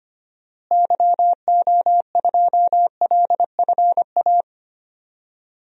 Morse code, heard as a keyed tone, about 25 wpm, 700 Hz. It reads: YO2LFA
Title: DXCC-100 Call Signs - 6 Characters - Repeated 2x - 25wpm